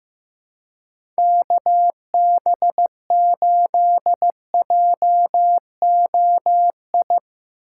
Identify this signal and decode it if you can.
Morse code, heard as a keyed tone, about 15 wpm, 700 Hz. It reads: KB8JOI
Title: USA Call Signs - 6 Characters - 8x Character Spacing - 15wpm